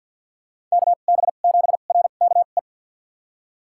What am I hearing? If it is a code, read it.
Morse code, heard as a keyed tone, about 47 wpm, spaced as for 35 wpm, 700 Hz. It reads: KB7RKE
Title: USA Call Signs - 2x Character Spacing - 50wpm